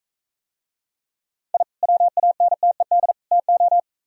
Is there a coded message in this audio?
Yes: Morse code from 1.5 to 3.8 s, heard as a keyed tone, about 44 wpm, 700 Hz. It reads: I WANTED TO